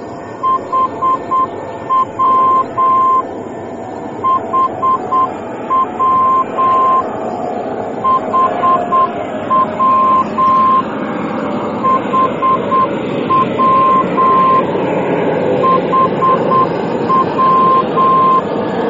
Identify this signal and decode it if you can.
Morse code, heard as a keyed tone, about 8 wpm, 1050 Hz, with noise. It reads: HW HW HW HW HW